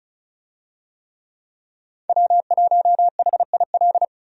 Morse code, heard as a keyed tone, about 36 wpm, 700 Hz. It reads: W1HIL
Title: USA Call Signs - 5 Characters - 35wpm